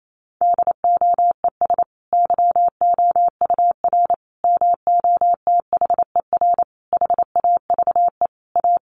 Morse code, heard as a keyed tone, about 28 wpm, 700 Hz. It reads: DOES YOUR MOTHER HAVE A
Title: Sentences from Top 1400 Words 28wpm